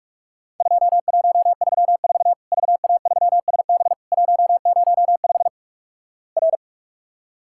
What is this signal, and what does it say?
Morse code, heard as a keyed tone, about 47 wpm, 700 Hz. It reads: 2134 VA3SB 105 R
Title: Worked All Europe DX Contest - QTC Exchange - 45wpm